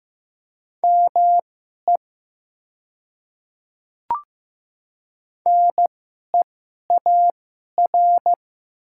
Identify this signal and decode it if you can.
Morse code, heard as a keyed tone, about 15 wpm, spaced as for 11 wpm, 700 Hz. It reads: ME NEAR
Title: 2x Character Spacing - Top 300 Words - 15wpm